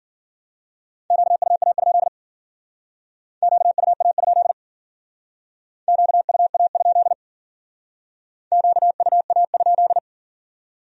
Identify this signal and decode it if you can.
Morse code, heard as a keyed tone, about 49 wpm, 700 Hz. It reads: QUA? QUA? QUA? QUA?
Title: Speed-Racing Q-Codes - 40wpm